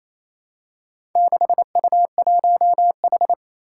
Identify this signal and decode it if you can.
Morse code, heard as a keyed tone, about 28 wpm, 700 Hz. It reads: 6U1H
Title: Four Challenging Letter-Number Combinations 28wpm